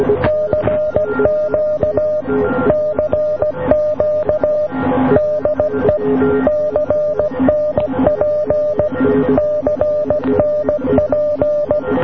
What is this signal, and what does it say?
Morse code, heard as a keyed tone, about 16 wpm, 590 Hz, with noise. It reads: CQ CQ DE CNP CNP